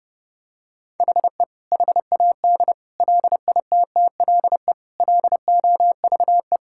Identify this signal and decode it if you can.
Morse code, heard as a keyed tone, about 32 wpm, 700 Hz. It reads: HE HAD LITTLE LOVE
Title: Sentences from Top 1600 Words 30wpm